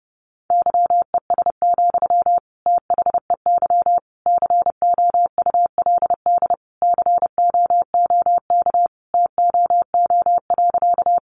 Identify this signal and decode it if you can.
Morse code, heard as a keyed tone, about 30 wpm, 700 Hz. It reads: YES, THEY COULD COOK TOO.
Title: Sentences from Top 700 Words 30wpm